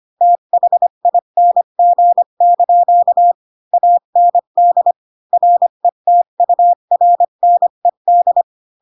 Morse code, heard as a keyed tone, about 25 wpm, 700 Hz. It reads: THING) AND RETURNED
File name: Pinocchio / Pinocchio ch3 20 25 ik1hge